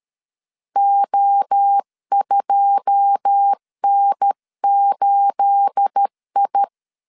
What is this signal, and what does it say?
Morse code, heard as a keyed tone, about 13 wpm, 800 Hz. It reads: O2N8I
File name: LEZIONI / 9 286 12wpm 2305